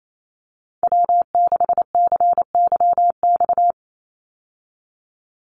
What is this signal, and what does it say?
Morse code, heard as a keyed tone, about 28 wpm, 690 Hz. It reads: W6CYX
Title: Call Signs Repeated 3x - 28wpm